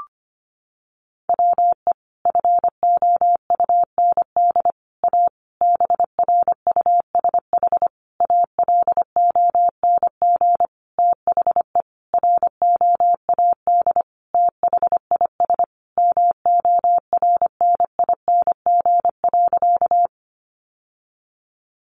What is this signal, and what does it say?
Morse code, heard as a keyed tone, about 25 wpm, 700 Hz. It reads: WE FOUND A BRUSH ALONG THE ROAD THIS MORNING.
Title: Sentences from Top 1100 Words 25wpm